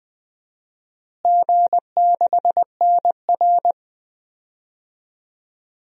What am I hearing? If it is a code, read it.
Morse code, heard as a keyed tone, about 20 wpm, 700 Hz. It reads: G6NR
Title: DXCC-100 Call Signs - 4 Characters - Repeated 3x - 20wpm